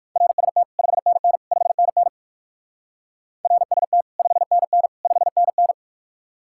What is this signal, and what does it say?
Morse code, heard as a keyed tone, about 64 wpm, 700 Hz. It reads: RST 5NN 5NN RST 5NN 5NN